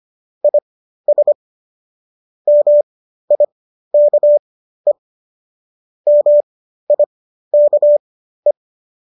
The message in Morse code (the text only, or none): IS MIKE MIKE